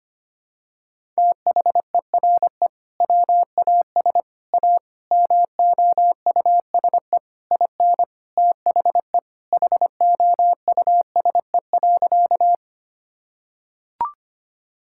Morse code, heard as a keyed tone, about 25 wpm, 700 Hz. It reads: THERE WAS A MOUSE IN THE HOUSE.
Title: Sentences from Top 1000 Words - Review Part 5 - 25wpm